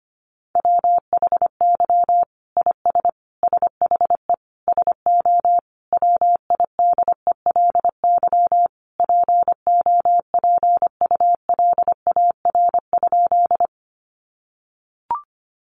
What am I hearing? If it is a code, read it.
Morse code, heard as a keyed tone, about 25 wpm, 700 Hz. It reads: WHY IS SHE SO WIDELY POPULAR?
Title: Sentences from Top 1200 Words 25wpm